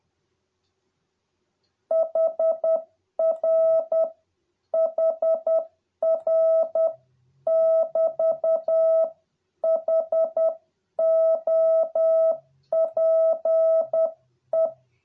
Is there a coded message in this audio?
Yes: Morse code from 1.9 to 14.7 s, heard as a keyed tone, about 10 wpm, 650 Hz. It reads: HR HR <BT> HOPE